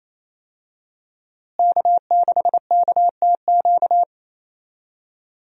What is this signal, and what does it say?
Morse code, heard as a keyed tone, about 27 wpm, 700 Hz. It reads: K6KTQ
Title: USA Call Signs - 5 Characters - Repeated 3x - 28wpm